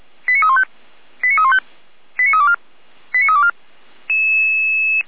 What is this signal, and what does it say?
Five-tone selective call (ZVEI), heard as quick runs of five tones, tones of about 70 ms. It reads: 89315 89315 89324 89324
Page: SDR/ZVEI